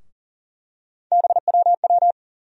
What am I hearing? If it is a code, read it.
Morse code, heard as a keyed tone, about 40 wpm, 700 Hz. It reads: BWW
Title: USA Call Sign Postfixes - 3 Letters 40wpm